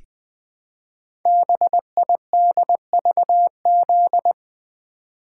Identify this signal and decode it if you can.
Morse code, heard as a keyed tone, about 20 wpm, 700 Hz. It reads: BIDVZ
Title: Code Groups - 5 Letter:Numbers - 20wpm